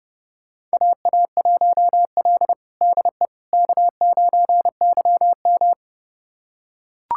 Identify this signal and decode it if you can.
Morse code, heard as a keyed tone, about 30 wpm, 700 Hz. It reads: AA1L DE K9YM
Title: USA Callsign DE Callsign - 4 Characters - 30wpm